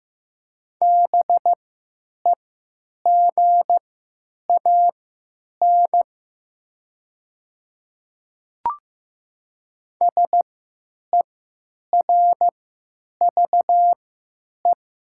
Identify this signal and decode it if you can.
Morse code, heard as a keyed tone, about 15 wpm, spaced as for 9 wpm, 700 Hz. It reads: BEGAN SERVE
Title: 3x Character Spacing - Top 500 Words - 15wpm